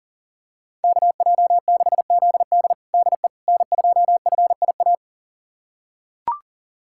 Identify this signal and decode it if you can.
Morse code, heard as a keyed tone, about 37 wpm, 700 Hz. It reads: KJ6ZD DE N2FIA
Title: USA Callsign DE Callsign - 5 Characters - 40wpm